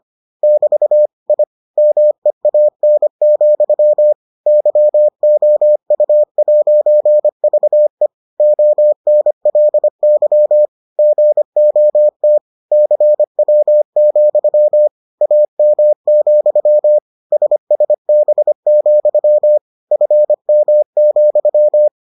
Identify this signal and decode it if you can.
Morse code, heard as a keyed tone, about 25 wpm, 600 Hz. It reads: <BT> I MEAN, YOU'VE ONLY GOT CW, AM, SSB, FM,